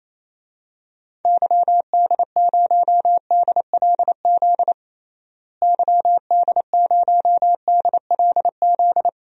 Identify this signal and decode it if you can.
Morse code, heard as a keyed tone, about 28 wpm, 700 Hz. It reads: YD0DLZ YD0DLZ
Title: DXCC-100 Call Signs - 6 Characters - Repeated 3x - 28wpm